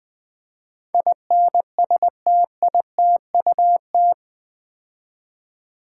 Morse code, heard as a keyed tone, about 20 wpm, 700 Hz. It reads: INSTITUT
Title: Build A Word - 4 Letters At A Time - 11-Letter Words - 20wpm